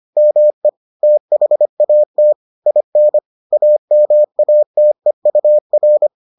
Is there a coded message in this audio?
Yes: Morse code from 0.2 to 6.1 s, heard as a keyed tone, about 25 wpm, 600 Hz. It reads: ME THAT IN AMATEUR